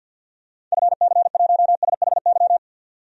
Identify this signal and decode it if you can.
Morse code, heard as a keyed tone, about 50 wpm, 700 Hz. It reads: FC1SHY
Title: DXCC-100 Call Signs - 6 Characters - 50wpm